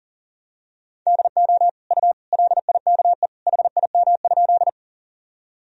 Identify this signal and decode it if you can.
Morse code, heard as a keyed tone, about 37 wpm, 700 Hz. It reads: DO U LIKE HIM?
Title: Sentences from Top 100 Words and Abbreviations 40wpm